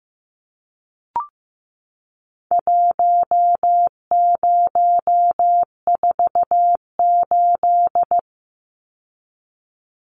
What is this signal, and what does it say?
Morse code, heard as a keyed tone, about 15 wpm, 700 Hz. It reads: 1048
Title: Numbers - Contest Style - 4-Digit 15wpm